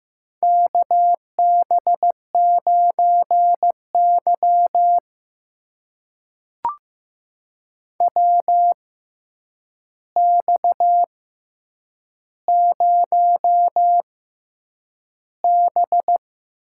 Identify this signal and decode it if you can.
Morse code, heard as a keyed tone, about 15 wpm, spaced as for 8 wpm, 700 Hz. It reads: KB9Y WX0B
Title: USA Call Signs - 6x Character Spacing - 15wpm